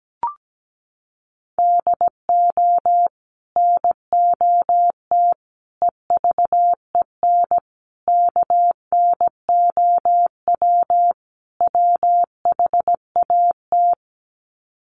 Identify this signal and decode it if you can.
Morse code, heard as a keyed tone, about 17 wpm, 700 Hz. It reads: DO NOT EVEN KNOW WHAT
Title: Sets of 5 Words Encore 17wpm